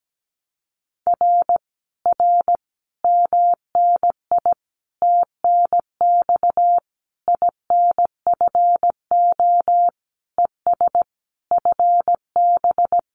Morse code, heard as a keyed tone, about 17 wpm, 700 Hz. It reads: R R MNI TNX INFO ES FB
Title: Common QSO Phrases Encore 17wpm